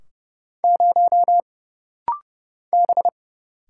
Morse code, heard as a keyed tone, about 30 wpm, 700 Hz. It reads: 0 B
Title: Single Letter-Number Alt - Rapid-Fire 30wpm